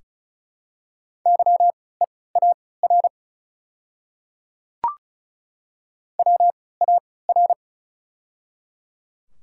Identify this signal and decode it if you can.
Morse code, heard as a keyed tone, about 36 wpm, spaced as for 20 wpm, 700 Hz. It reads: YEAR WAR
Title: 3x Character Spacing - Top 500 Words - 35wpm